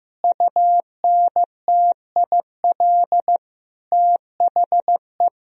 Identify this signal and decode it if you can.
Morse code, heard as a keyed tone, about 16 wpm, 700 Hz. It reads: UNTIL THE
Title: Sentences from Top 1000 Words - Review Part 1 - 15wpm